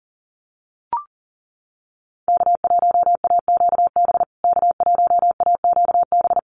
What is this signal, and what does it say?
Morse code, heard as a keyed tone, about 40 wpm, 700 Hz. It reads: K1AQB K1AQB
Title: USA Call Signs Repeated 3x 40wpm